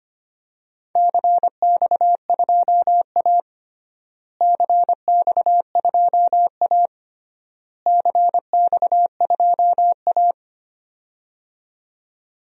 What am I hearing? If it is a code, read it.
Morse code, heard as a keyed tone, about 25 wpm, 700 Hz. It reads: CX2A CX2A CX2A
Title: DXCC-100 Call Signs - 4 Characters - Repeated 3x - 25wpm